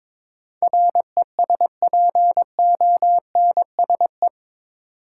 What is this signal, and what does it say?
Morse code, heard as a keyed tone, about 21 wpm, 700 Hz. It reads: RESPONSE
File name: practice / 22wpm / Eight Letter Words - 22wpm